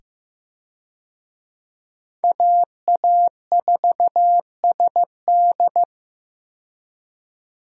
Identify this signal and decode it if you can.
Morse code, heard as a keyed tone, about 15 wpm, 700 Hz. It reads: AA4SD
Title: USA Call Signs - 5 Characters - 15wpm